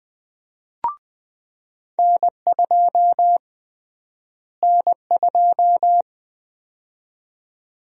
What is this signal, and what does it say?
Morse code, heard as a keyed tone, about 20 wpm, 700 Hz. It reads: N2 N2
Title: USA Call Sign Prefixes - Repeated 2x - 20wpm